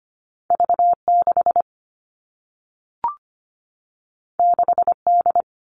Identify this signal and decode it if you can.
Morse code, heard as a keyed tone, about 25 wpm, 700 Hz. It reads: V 6 6 D
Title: Two Challenging Letter-Number Combinations 25wpm